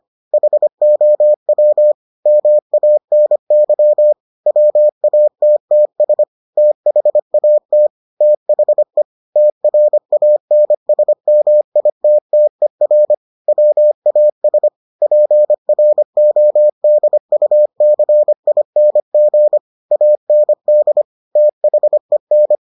Morse code, heard as a keyed tone, about 25 wpm, 600 Hz. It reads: HOW MANY WATTS THAT THE TRANSMITTER WAS PRODUCING AND THEN